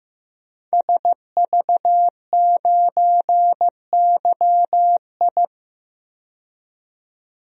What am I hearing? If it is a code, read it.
Morse code, heard as a keyed tone, about 15 wpm, 700 Hz. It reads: SV9YI
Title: DXCC-100 Call Signs - 5 Characters - 15wpm